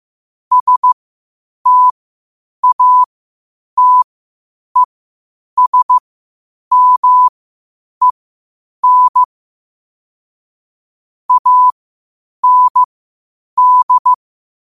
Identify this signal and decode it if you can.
Morse code, heard as a keyed tone, about 15 wpm, spaced as for 8 wpm, 1000 Hz. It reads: STATESMEN AND
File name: qotdfiles / 8wpm